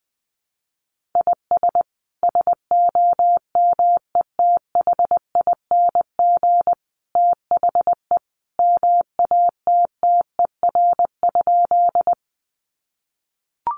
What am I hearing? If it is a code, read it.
Morse code, heard as a keyed tone, about 20 wpm, 700 Hz. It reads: IS SOMETHING THE MATTER?
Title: Sentences from Top 700 Words 20wpm